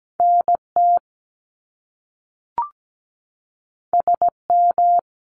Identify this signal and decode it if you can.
Morse code, heard as a keyed tone, about 17 wpm, 700 Hz. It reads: NT SM
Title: Top Two Letter Combinations in Words 17wpm